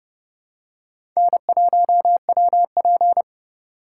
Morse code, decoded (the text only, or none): N1WP